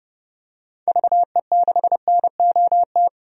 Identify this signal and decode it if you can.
Morse code, heard as a keyed tone, about 31 wpm, 700 Hz. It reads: VE6NOT
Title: DXCC-100 Call Signs - 6 Characters - 30wpm